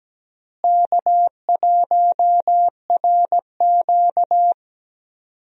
Morse code, heard as a keyed tone, about 17 wpm, 700 Hz. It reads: K1RQ
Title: USA Call Signs - 4 Characters - Repeated 2x - 17wpm